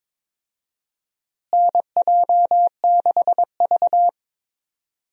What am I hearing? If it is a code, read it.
Morse code, heard as a keyed tone, about 22 wpm, 700 Hz. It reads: NJ6V
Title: USA Call Signs - 4 Characters - Repeated 2x - 22wpm